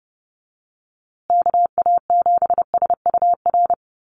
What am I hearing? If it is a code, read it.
Morse code, heard as a keyed tone, about 30 wpm, 700 Hz. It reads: KA7SUR
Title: USA Call Signs Repeated 2x 30wpm